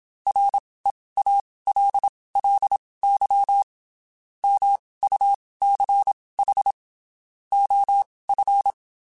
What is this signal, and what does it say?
Morse code, heard as a keyed tone, about 27 wpm, spaced as for 19 wpm, 800 Hz. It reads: REALLY MUCH OF
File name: (spaced) / QSOs Spaced-mp3s / WD0BGP 1 @25 sp